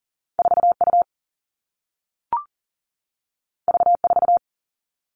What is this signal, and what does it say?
Morse code, heard as a keyed tone, about 42 wpm, 700 Hz. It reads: 4U V4
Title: Two Challenging Letter-Number Combinations 40wpm